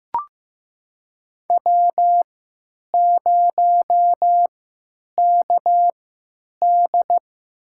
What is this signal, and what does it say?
Morse code, heard as a keyed tone, about 15 wpm, spaced as for 9 wpm, 700 Hz. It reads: W0KD
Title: USA Call Signs - 4 Characters - 3x Character Spacing - 15wpm